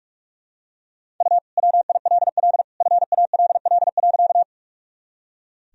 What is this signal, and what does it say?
Morse code, heard as a keyed tone, about 42 wpm, 700 Hz. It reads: U WILL FALL.
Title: Sentences from Top 500 Words and Abbreviations 45wpm